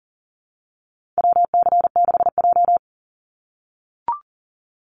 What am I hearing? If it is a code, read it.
Morse code, heard as a keyed tone, about 40 wpm, 700 Hz. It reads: WC6J
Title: NA SOTA Activator Callsigns 40wpm